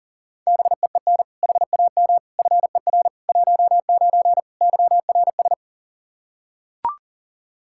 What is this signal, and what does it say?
Morse code, heard as a keyed tone, about 40 wpm, 700 Hz. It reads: BEEN HAM FER 19 YRS
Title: Practice QSO - Exchange 3 - More Info - 40wpm